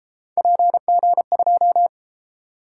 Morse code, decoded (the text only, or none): PG2